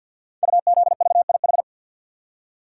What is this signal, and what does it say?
Morse code, heard as a keyed tone, about 48 wpm, 700 Hz. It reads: UZVIH
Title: Code Groups - 5 Letters - 50wpm